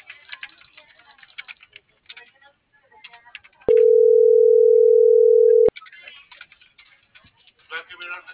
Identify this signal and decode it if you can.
Ringback tone (North America), 440 + 480 Hz.